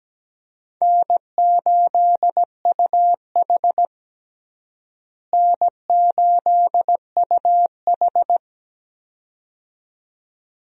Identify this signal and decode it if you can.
Morse code, heard as a keyed tone, about 17 wpm, 700 Hz. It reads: N8UH N8UH
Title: USA Call Signs - 4 Characters - Repeated 2x - 17wpm